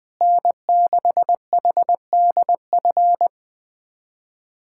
Morse code, heard as a keyed tone, about 20 wpm, 700 Hz. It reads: N6HDF
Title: USA Callsign DE Callsign 20wpm